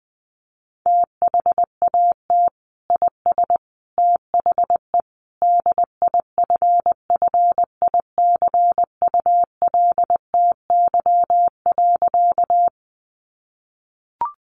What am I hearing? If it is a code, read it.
Morse code, heard as a keyed tone, about 20 wpm, 700 Hz. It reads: THAT IS THE DIFFICULTY.
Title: Sentences from Top 1500 Words 20wpm